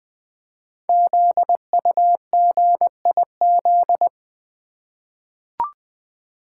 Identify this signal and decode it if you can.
Morse code, heard as a keyed tone, about 20 wpm, 700 Hz. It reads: ZUGIZ
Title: Code Groups - 5 Letter:Numbers - 20wpm